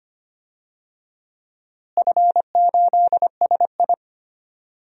Morse code, heard as a keyed tone, about 25 wpm, 700 Hz. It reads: F8SI
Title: DXCC-100 Call Signs - 4 Characters - Repeated 3x - 25wpm